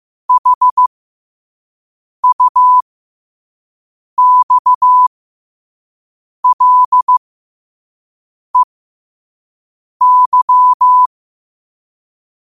Morse code, heard as a keyed tone, about 15 wpm, spaced as for 5 wpm, 1000 Hz. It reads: HUXLEY
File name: qotdfiles / 5wpm